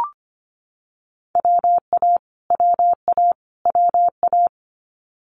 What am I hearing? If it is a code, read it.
Morse code, heard as a keyed tone, about 25 wpm, 700 Hz. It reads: WA WA WA
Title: Multi-letter Phonemes Repeated 3x 25wpm